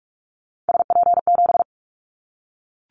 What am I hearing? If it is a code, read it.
Morse code, heard as a keyed tone, about 44 wpm, 700 Hz. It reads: SP7
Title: Callsign Prefixes for Top 25 DXCC - Repeated 3x - 45wpm